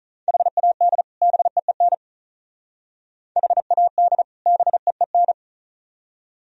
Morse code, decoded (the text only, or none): HAD BEEN HAD BEEN